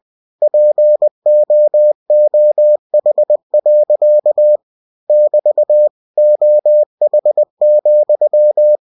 Morse code, heard as a keyed tone, about 20 wpm, 600 Hz. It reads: POOH. <BT> OH,
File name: cw-books / winnie-the-pooh / winnie-20wpm-page-0109